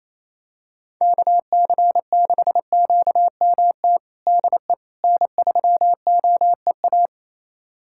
Morse code, heard as a keyed tone, about 28 wpm, 700 Hz. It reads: KC6QMT DE N3OEA
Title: USA Callsign DE Callsign 28wpm